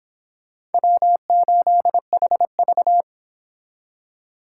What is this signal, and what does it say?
Morse code, heard as a keyed tone, about 29 wpm, 700 Hz. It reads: W8HV